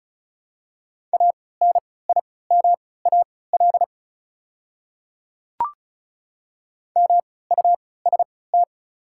Morse code, heard as a keyed tone, about 36 wpm, spaced as for 20 wpm, 700 Hz. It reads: ANIMAL MUST
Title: 3x Character Spacing - Top 200 Words - 35wpm